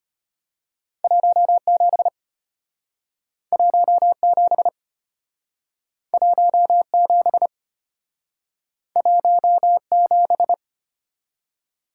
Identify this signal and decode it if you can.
Morse code, heard as a keyed tone, about 34 wpm, 700 Hz. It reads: 17 17 17 17